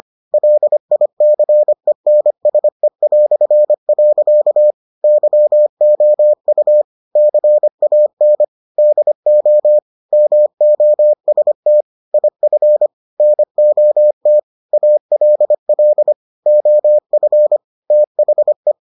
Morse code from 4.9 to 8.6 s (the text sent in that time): YOU CAN